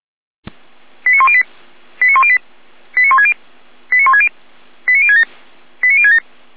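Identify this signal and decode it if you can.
Five-tone selective call (ZVEI), heard as quick runs of five tones, tones of about 70 ms. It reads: 89108 89108 89160 89160 89067 89067
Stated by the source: Five-tone ZVEI selective call received on an SDR.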